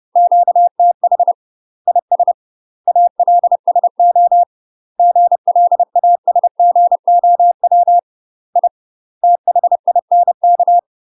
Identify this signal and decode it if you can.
Morse code, heard as a keyed tone, about 30 wpm, 700 Hz. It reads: QTH IS ALSO GLASGOW I THINK
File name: UK MASTER QSOs TRAINING / 30WPM / UK MASTER QSOs 0004